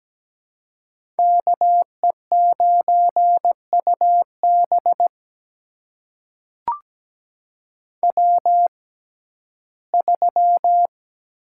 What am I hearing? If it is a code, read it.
Morse code, heard as a keyed tone, about 17 wpm, spaced as for 12 wpm, 700 Hz. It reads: KE9UB W3
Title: USA Call Signs - 5 Characters - 6x Character Spacing - 17wpm